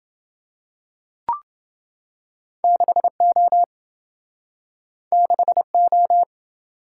Morse code, heard as a keyed tone, about 28 wpm, 700 Hz. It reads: 6O 6O